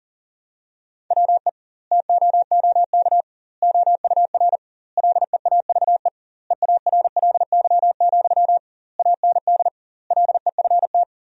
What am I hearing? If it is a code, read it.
Morse code, heard as a keyed tone, about 40 wpm, 700 Hz. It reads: WE TOOK OUR LEAVE EARLY, AND LEFT
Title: 2x Word-Spacing - Sentences from Top 400 Words - 40wpm